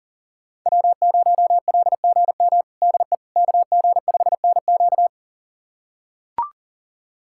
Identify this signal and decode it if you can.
Morse code, heard as a keyed tone, about 40 wpm, 700 Hz. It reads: W0LGM DE KG5NQ
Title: USA Callsign DE Callsign - 5 Characters - 40wpm